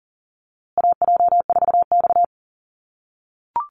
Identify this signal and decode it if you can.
Morse code, heard as a keyed tone, about 40 wpm, 700 Hz. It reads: AJ4X
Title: USA Call Signs Repeated 2x 40wpm